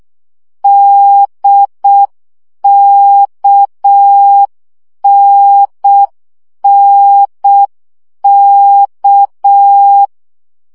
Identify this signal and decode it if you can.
Morse code, heard as a keyed tone, about 6 wpm, 800 Hz. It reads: DKNNK
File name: CW gyakorlas n d k 5 baud